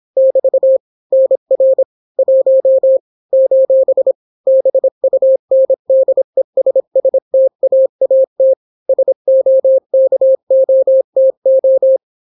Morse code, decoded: <BT> NR 1 : BUNDESSTAAT SOKOTO